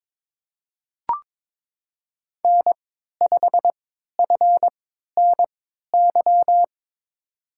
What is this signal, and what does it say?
Morse code, heard as a keyed tone, about 22 wpm, spaced as for 13 wpm, 700 Hz. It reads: N5FNY
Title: USA Call Signs - 5 Characters - 3x Character Spacing - 22wpm